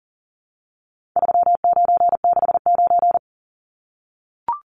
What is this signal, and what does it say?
Morse code, heard as a keyed tone, about 40 wpm, 700 Hz. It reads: 3969
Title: Numbers - 4-Digit 40wpm